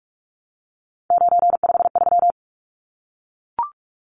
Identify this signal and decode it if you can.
Morse code, heard as a keyed tone, about 44 wpm, 700 Hz. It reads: 953
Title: Numbers - 3-Digit 45wpm